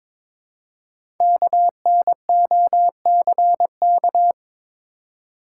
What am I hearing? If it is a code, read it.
Morse code, heard as a keyed tone, about 22 wpm, 700 Hz. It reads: KNOCK
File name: 5 Letter Words - First 3 Letters Given - 22wpm